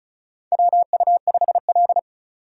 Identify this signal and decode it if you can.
Morse code, heard as a keyed tone, about 36 wpm, 690 Hz. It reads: WU5L
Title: USA Call Signs - 4x Character Spacing - 35wpm